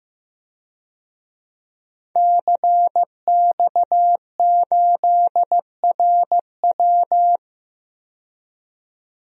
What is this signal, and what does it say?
Morse code, heard as a keyed tone, about 15 wpm, 700 Hz. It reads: CX8RW
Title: DXCC-100 Call Signs - 5 Characters - 15wpm